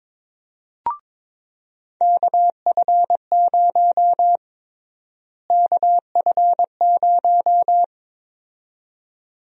Morse code, heard as a keyed tone, about 22 wpm, 700 Hz. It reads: KF0 KF0
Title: USA Call Sign Prefixes - Repeated 2x - 22wpm